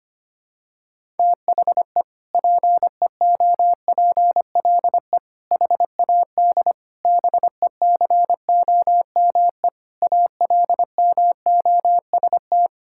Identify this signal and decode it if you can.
Morse code, heard as a keyed tone, about 25 wpm, 700 Hz. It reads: THE PEOPLE HAD BECOME ALMOST
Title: Sentences from Top 1000 Words - Review Part 6 - 25wpm